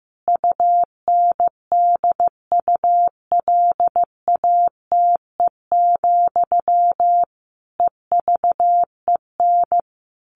Morse code, heard as a keyed tone, about 14 wpm, 700 Hz. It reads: UNDULATE, EVEN
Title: A Sound Of Thunder 15wpm